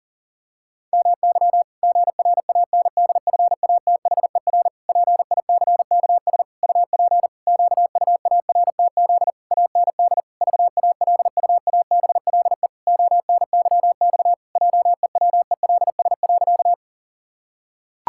Morse code, heard as a keyed tone, about 40 wpm, 700 Hz. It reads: MY GRANDFATHER PICKS UP QUARTZ AND VALUABLE ONYX JEWELS.
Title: Pangram Challenge v2 - 40wpm